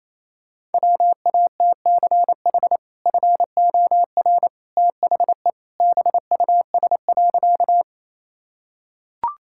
Morse code, heard as a keyed tone, about 28 wpm, 700 Hz. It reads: WATCH FOR THE BUS.
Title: Sentences from Top 1000 Words - Review Part 6 - 28wpm